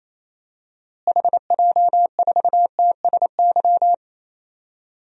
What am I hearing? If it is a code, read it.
Morse code, heard as a keyed tone, about 27 wpm, 700 Hz. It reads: HJ4TSY
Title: DXCC-100 Call Signs - 6 Characters - Repeated 3x - 28wpm